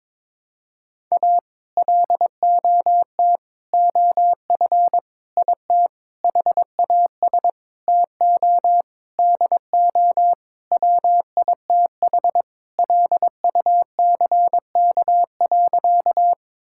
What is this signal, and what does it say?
Morse code, heard as a keyed tone, about 22 wpm, 700 Hz. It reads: A LOT OF IT HAS TO DO WITH LUCK.
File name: practice / 22wpm / Sentences with 8-Word Phrases - 22wpm